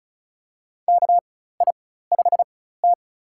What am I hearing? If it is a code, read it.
Morse code, heard as a keyed tone, about 34 wpm, spaced as for 16 wpm, 700 Hz. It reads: KI5T
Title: USA Call Signs - 4 Characters - 4x Character Spacing - 35wpm